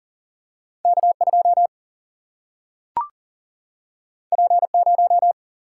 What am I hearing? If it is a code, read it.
Morse code, heard as a keyed tone, about 40 wpm, 700 Hz. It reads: K2 P0
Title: Single-Letter Single-Number 40wpm